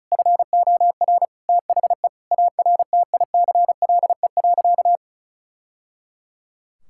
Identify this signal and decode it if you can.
Morse code, heard as a keyed tone, about 34 wpm, 700 Hz. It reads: FOR THE ARTICLE.
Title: Sentences from Top 1000 Words - Review Part 5 - 35wpm